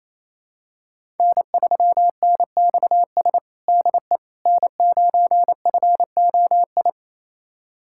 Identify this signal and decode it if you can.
Morse code, heard as a keyed tone, about 28 wpm, 700 Hz. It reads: N3NXS DE N9FOI
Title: USA Callsign DE Callsign - 5 Characters - 28wpm